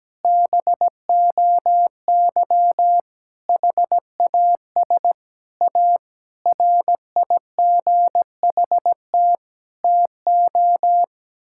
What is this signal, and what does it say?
Morse code, heard as a keyed tone, about 17 wpm, 700 Hz. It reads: BOY HAS A RIGHT TO
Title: Sentences from Top 1000 Words - Review Part 6 - 17wpm